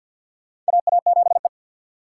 Morse code, heard as a keyed tone, about 53 wpm, 700 Hz. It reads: AA7E
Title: USA Call Signs - 4 Characters - 2x Character Spacing - 50wpm